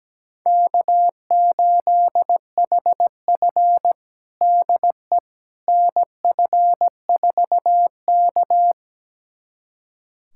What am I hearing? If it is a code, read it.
Morse code, heard as a keyed tone, about 17 wpm, 700 Hz. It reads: K8HF DE NF4K
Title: USA Callsign DE Callsign - 4 Characters - 17wpm